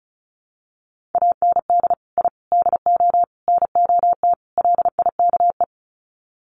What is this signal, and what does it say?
Morse code, heard as a keyed tone, about 34 wpm, 700 Hz. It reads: AND I DO NOT LIKE
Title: Sets of 5 Words Encore2 35wpm